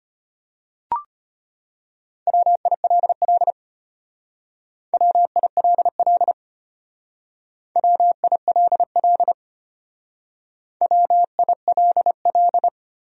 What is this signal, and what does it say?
Morse code, heard as a keyed tone, about 32 wpm, 700 Hz. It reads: WILL WILL WILL WILL